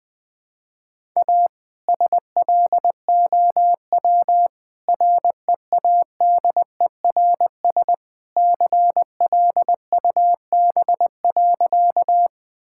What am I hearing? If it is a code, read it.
Morse code, heard as a keyed tone, about 20 wpm, 700 Hz. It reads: A SLOW READERS CLUB.
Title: All-Age Appropriate Jokes - 20wpm